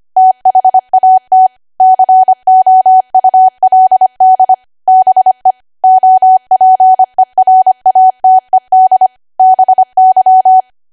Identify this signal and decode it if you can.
Morse code, heard as a keyed tone, about 25 wpm, 750 Hz. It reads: THAT COULD BE OPERATED BY
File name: cwa / W1AW-Level3 / 25WPM